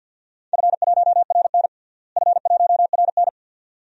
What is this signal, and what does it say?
Morse code, heard as a keyed tone, about 53 wpm, 700 Hz. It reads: F1RN F1RN
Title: DXCC-100 Call Signs - 4 Characters - Repeated 2x - 50wpm